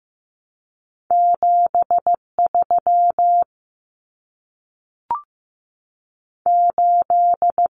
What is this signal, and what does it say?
Morse code, heard as a keyed tone, about 15 wpm, 700 Hz. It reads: 7 3 8
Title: Common Cut Numbers - 2-Digit 15wpm